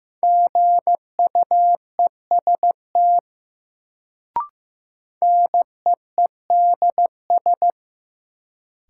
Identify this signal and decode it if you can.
Morse code, heard as a keyed tone, about 15 wpm, 700 Hz. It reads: GUEST NEEDS
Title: Sentences from Top 1600 Words - New Word Drill Alt - 15wpm